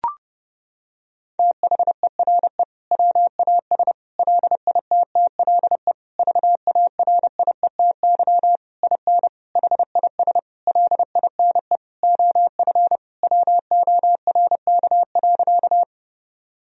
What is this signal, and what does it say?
Morse code, heard as a keyed tone, about 30 wpm, 700 Hz. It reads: THERE WAS LITTLE VARIETY IN HIS LINE OF WORK.
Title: Sentences from Top 1000 Words - Review Part 4 - 30wpm